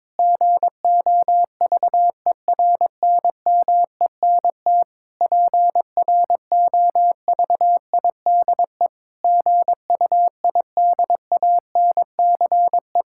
Morse code, heard as a keyed tone, about 22 wpm, 700 Hz. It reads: GOVERNMENT PROVIDE GUIDANCE
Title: Sentences from Top 1600 Words 22wpm